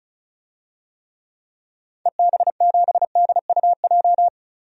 Morse code, heard as a keyed tone, about 34 wpm, 700 Hz. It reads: EB7DUJ
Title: DXCC-100 Call Signs - 6 Characters - 35wpm